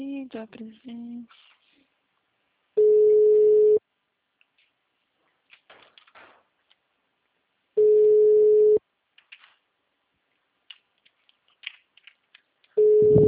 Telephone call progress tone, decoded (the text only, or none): ringback tone